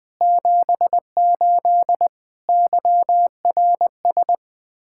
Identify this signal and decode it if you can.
Morse code, heard as a keyed tone, about 20 wpm, 700 Hz. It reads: 78 YRS
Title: Practice QSO - Exchange 3 - More Info - 20wpm